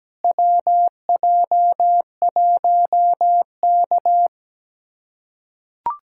Morse code, heard as a keyed tone, about 17 wpm, 700 Hz. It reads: WJ1K
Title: USA Callsign DE Callsign - 4 Characters - 17wpm